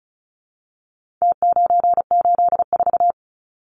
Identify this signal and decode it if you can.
Morse code, heard as a keyed tone, about 34 wpm, 700 Hz. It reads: T984
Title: Numbers - Contest Style - 4-Digit 35wpm